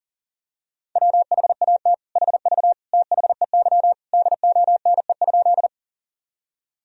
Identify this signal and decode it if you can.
Morse code, heard as a keyed tone, about 37 wpm, 700 Hz. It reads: WHAT HV THEY DONE?
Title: Sentences from Top 500 Words and Abbreviations 40wpm